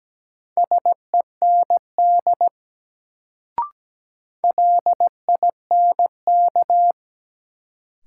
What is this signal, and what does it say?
Morse code, heard as a keyed tone, about 17 wpm, 700 Hz. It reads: SEND LINK
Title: Sentences from Top 1500 Words - New Word Drill Alt - 17wpm